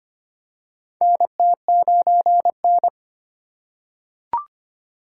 Morse code, heard as a keyed tone, about 25 wpm, 700 Hz. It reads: NT9N
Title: USA Call Signs - 4 Characters - 4x Character Spacing - 25wpm